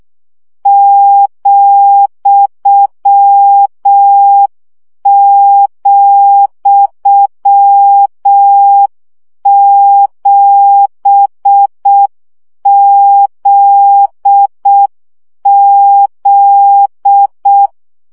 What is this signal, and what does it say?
Morse code, heard as a keyed tone, about 6 wpm, 800 Hz. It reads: ,,7ZZ
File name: CW gyakorlas z 7 vesszo 5 baud